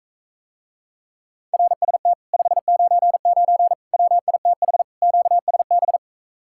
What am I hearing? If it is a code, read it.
Morse code, heard as a keyed tone, about 44 wpm, 700 Hz. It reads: RST 599 WITH QSB